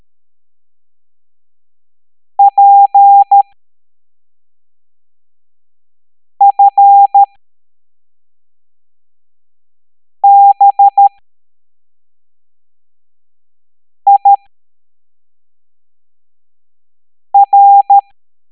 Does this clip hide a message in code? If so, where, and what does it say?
Yes: Morse code from 2.4 to 18.0 s, heard as a keyed tone, about 13 wpm, spaced as for 3 wpm, 800 Hz. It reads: PFBIR